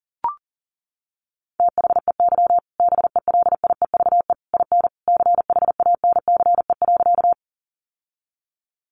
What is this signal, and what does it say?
Morse code, heard as a keyed tone, about 40 wpm, 700 Hz. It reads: THEY BELIEVE IN CHANCE.
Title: Sentences from Top 1000 Words 40wpm